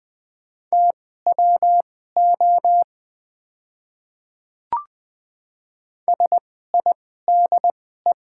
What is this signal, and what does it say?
Morse code, heard as a keyed tone, about 21 wpm, spaced as for 15 wpm, 700 Hz. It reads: TWO SIDE
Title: 2x Character Spacing - Top 100 Words - 20wpm